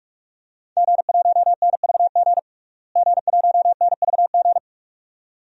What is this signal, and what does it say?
Morse code, heard as a keyed tone, about 46 wpm, 700 Hz. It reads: G1NVG G1NVG
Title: DXCC-100 Call Signs - 5 Characters - Repeated 2x - 45wpm